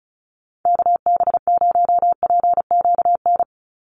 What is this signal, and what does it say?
Morse code, heard as a keyed tone, about 34 wpm, 700 Hz. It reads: KB0PQN
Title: USA Call Signs 35wpm